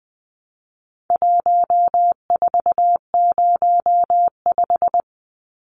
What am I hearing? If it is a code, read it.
Morse code, heard as a keyed tone, about 20 wpm, 700 Hz. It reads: 1405
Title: Numbers - 4-Digit 20wpm